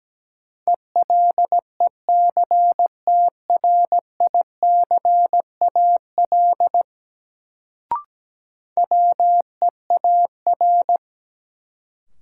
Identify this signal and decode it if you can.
Morse code, heard as a keyed tone, about 17 wpm, 700 Hz. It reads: ELECTRICAL WEAR
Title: Sentences from Top 1000 Words - Review Part 6 - New Word Drill - 17wpm